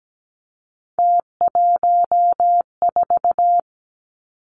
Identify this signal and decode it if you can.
Morse code, heard as a keyed tone, about 17 wpm, 700 Hz. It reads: T14
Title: Three Challenging Letter-Number Combinations 17wpm